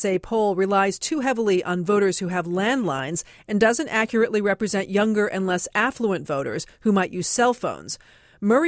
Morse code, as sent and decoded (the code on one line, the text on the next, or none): none